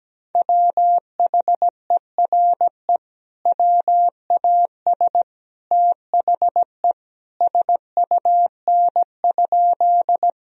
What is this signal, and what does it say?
Morse code, heard as a keyed tone, about 17 wpm, 700 Hz. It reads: WHERE WAS THE SUN?
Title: Sentences from Top 1000 Words - Review Part 1 - 17wpm